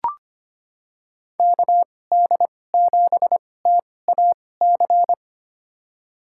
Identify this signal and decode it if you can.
Morse code, heard as a keyed tone, about 25 wpm, spaced as for 18 wpm, 700 Hz. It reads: KD7TAC
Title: USA Call Signs - 6 Characters - 2x Character Spacing - 25wpm